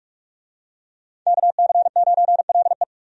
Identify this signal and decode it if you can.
Morse code, heard as a keyed tone, about 44 wpm, 700 Hz. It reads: KC9LE
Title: USA Call Signs - 5 Characters - 45wpm